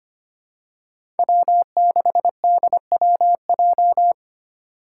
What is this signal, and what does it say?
Morse code, heard as a keyed tone, about 25 wpm, 700 Hz. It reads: W6DWJ
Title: Five Challenging Letter-Number Combinations 25wpm